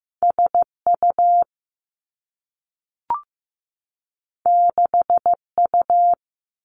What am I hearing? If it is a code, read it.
Morse code, heard as a keyed tone, about 15 wpm, 700 Hz. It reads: SU 6U
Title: Two Challenging Letter-Number Combinations 15wpm